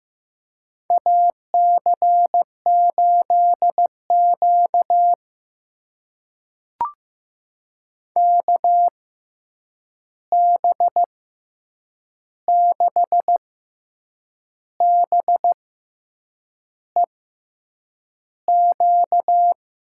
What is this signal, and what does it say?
Morse code, heard as a keyed tone, about 15 wpm, spaced as for 7 wpm, 700 Hz. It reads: AC8Q KB6BEQ
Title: USA Call Signs - 6x Character Spacing - 15wpm